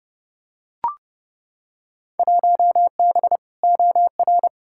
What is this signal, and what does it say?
Morse code, heard as a keyed tone, about 30 wpm, 700 Hz. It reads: 1B OR
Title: Field Day Exchanges Encore - 30wpm